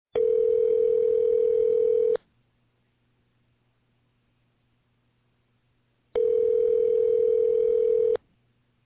Ringback tone (North America), 440 + 480 Hz.